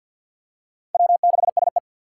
Morse code, heard as a keyed tone, about 52 wpm, 700 Hz. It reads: W6SE
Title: USA Call Signs - 4 Characters - 2x Character Spacing - 50wpm